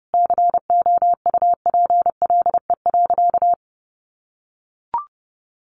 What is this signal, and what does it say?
Morse code, heard as a keyed tone, about 30 wpm, 700 Hz. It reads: COUPLE.
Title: Sentences from Top 1100 Words 30wpm